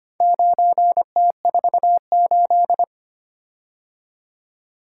Morse code, heard as a keyed tone, about 25 wpm, 700 Hz. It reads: 9T48
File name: practice / 25wpm / Four Letter-Number Combinations - 25wpm